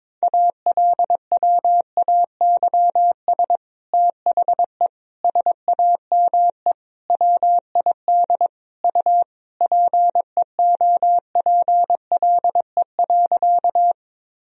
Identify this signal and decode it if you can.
Morse code, heard as a keyed tone, about 22 wpm, 700 Hz. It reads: ALWAYS THE SAME WID U PEOPLE.
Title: Sentences from Top 300 Words and Abbreviations 22wpm